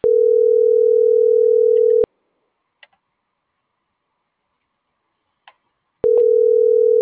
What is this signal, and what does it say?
Ringback tone (North America), 440 + 480 Hz.